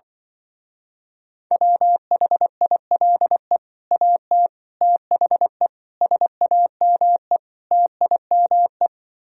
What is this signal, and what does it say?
Morse code, heard as a keyed tone, about 24 wpm, 700 Hz. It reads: WHILE AT THE SAME TIME
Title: Speed-Racing - Sets of 5 Words 20wpm